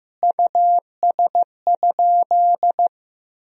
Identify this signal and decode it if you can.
Morse code, heard as a keyed tone, about 16 wpm, 700 Hz. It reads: US?
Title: Sentences from Top 400 Words and Abbreviations 15wpm